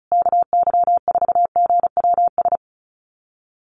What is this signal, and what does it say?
Morse code, heard as a keyed tone, about 34 wpm, 700 Hz. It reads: KY4GWS
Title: USA Call Signs Repeated 3x 35wpm